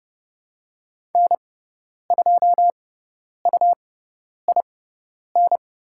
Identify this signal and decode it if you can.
Morse code, heard as a keyed tone, about 30 wpm, spaced as for 10 wpm, 700 Hz. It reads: N2UIN
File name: Lesson 18g - Letter m - Call Signs 30wpm